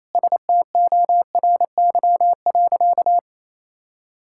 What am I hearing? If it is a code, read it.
Morse code, heard as a keyed tone, about 28 wpm, 700 Hz. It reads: STORY.
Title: Sentences from Top 1000 Words - Review Part 5 - 28wpm